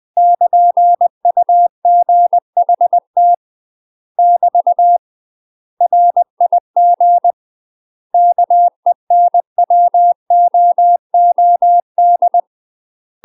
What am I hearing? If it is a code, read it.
Morse code, heard as a keyed tone, about 20 wpm, 700 Hz. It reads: <KN>UGHT <BT> RIG KENWOOD
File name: UK MASTER QSOs TRAINING / 20WPM / UK MASTER QSOs 0037